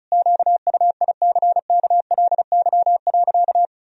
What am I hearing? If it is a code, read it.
Morse code, heard as a keyed tone, about 34 wpm, 700 Hz. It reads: QUICKLY.
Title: Sentences from Top 1000 Words - Review Part 1 - 35wpm